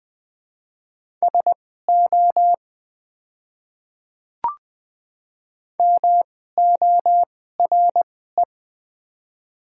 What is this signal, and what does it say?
Morse code, heard as a keyed tone, about 21 wpm, spaced as for 15 wpm, 700 Hz. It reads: SO MORE
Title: 2x Character Spacing - Top 100 Words - 20wpm